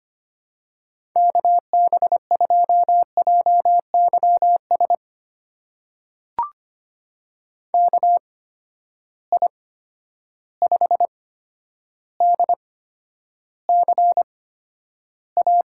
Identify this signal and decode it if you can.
Morse code, heard as a keyed tone, about 25 wpm, spaced as for 11 wpm, 700 Hz. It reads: KB2JYS KI5DCA
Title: USA Call Signs - 6 Characters - 8x Character Spacing - 25wpm